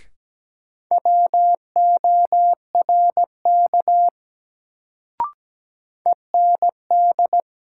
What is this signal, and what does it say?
Morse code, heard as a keyed tone, about 17 wpm, 700 Hz. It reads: WORK END
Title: Sentences from Top 200 Words - New Word Drill - 17wpm